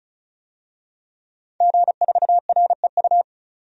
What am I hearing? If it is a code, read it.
Morse code, heard as a keyed tone, about 34 wpm, 700 Hz. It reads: G4REU
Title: DXCC-100 Call Signs - 5 Characters - 35wpm